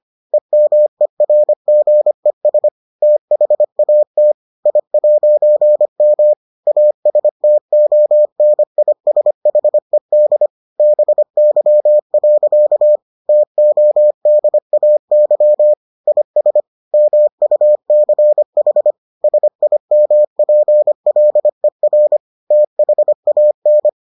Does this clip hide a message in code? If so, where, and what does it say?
Yes: Morse code from 0.3 to 23.9 s, heard as a keyed tone, about 25 wpm, 600 Hz. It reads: EMERGES THAT I'M ASTONISHED BY. TODAY IS MUCH SIMPLER THAN